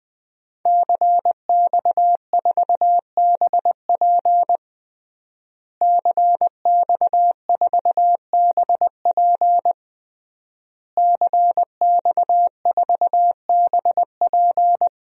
Morse code, heard as a keyed tone, about 21 wpm, 700 Hz. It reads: CX4BP CX4BP CX4BP
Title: DXCC-100 Call Signs - 5 Characters - Repeated 3x - 20wpm